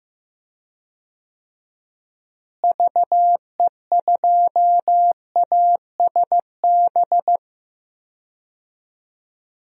Morse code, heard as a keyed tone, about 16 wpm, 700 Hz. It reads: VE2ASB
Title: DXCC-100 Call Signs - 6 Characters - Repeated 3x - 15wpm